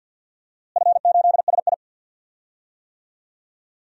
Morse code, heard as a keyed tone, about 51 wpm, 700 Hz. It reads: F8SI
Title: DXCC-100 Call Signs - 4 Characters - Repeated 2x - 50wpm